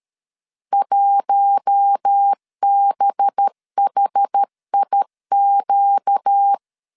Morse code, heard as a keyed tone, about 13 wpm, 800 Hz. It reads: 1BHIQ
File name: LEZIONI / 6 PKX0 / 6 P 12wpm 2033